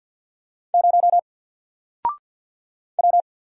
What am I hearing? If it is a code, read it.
Morse code, heard as a keyed tone, about 48 wpm, 700 Hz. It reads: 0 W
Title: Single Character (Letter:Number:Punctuation:Prosign) Alt 50wpm